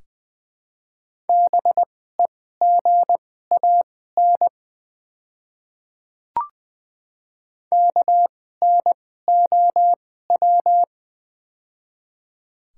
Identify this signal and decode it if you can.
Morse code, heard as a keyed tone, about 20 wpm, spaced as for 15 wpm, 700 Hz. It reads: BEGAN KNOW
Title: 2x Character Spacing - Top 500 Words - 20wpm